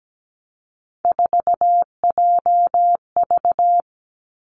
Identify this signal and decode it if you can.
Morse code, heard as a keyed tone, about 17 wpm, 700 Hz. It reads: 4JV
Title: Three Challenging Letter-Number Combinations 17wpm